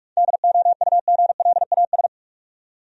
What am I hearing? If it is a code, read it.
Morse code, heard as a keyed tone, about 43 wpm, 700 Hz. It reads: DOUGLAS
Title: USA Common Names - 45wpm